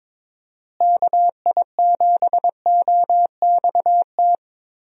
Morse code, heard as a keyed tone, about 22 wpm, 700 Hz. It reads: KI7OXT
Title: USA Call Signs - 6 Characters - Repeated 2x - 22wpm